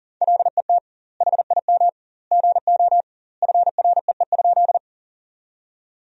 Morse code, heard as a keyed tone, about 37 wpm, 700 Hz. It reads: LET HIM GO FREE?
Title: 2x Word-Spacing - Sentences from Top 500 Words - 40wpm